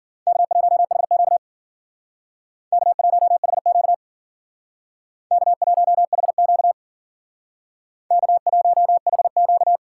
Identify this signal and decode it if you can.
Morse code, heard as a keyed tone, about 51 wpm, 700 Hz. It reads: K1HQ K1HQ K1HQ K1HQ